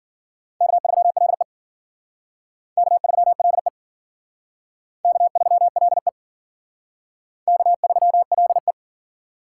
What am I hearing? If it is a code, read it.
Morse code, heard as a keyed tone, about 52 wpm, 700 Hz. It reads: K3LE K3LE K3LE K3LE